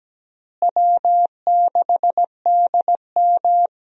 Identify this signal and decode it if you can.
Morse code, heard as a keyed tone, about 17 wpm, 700 Hz. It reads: W6DM
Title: USA Call Signs - 4 Characters - Repeated 3x - 17wpm